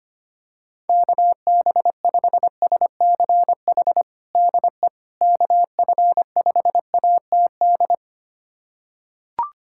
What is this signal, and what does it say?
Morse code, heard as a keyed tone, about 25 wpm, 700 Hz. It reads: KB5SCH DE KF5ATD
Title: USA Callsign DE Callsign - 6 Characters - 25wpm